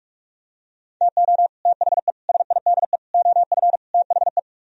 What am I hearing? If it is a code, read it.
Morse code, heard as a keyed tone, about 49 wpm, 700 Hz. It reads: TO THE SIDE OF THE